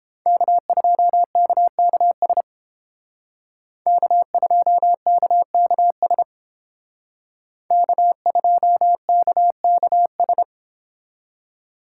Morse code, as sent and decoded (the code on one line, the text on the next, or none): -.- ..--- -.- -.- ... / -.- ..--- -.- -.- ... / -.- ..--- -.- -.- ...
K2KKS K2KKS K2KKS